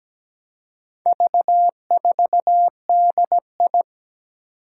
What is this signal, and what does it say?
Morse code, heard as a keyed tone, about 17 wpm, 700 Hz. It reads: V4DI
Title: Four Challenging Letter-Number Combinations 17wpm